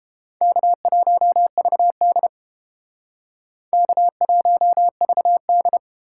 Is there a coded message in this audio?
Yes: Morse code from 0.4 to 5.8 s, heard as a keyed tone, about 31 wpm, 700 Hz. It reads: K1VD K1VD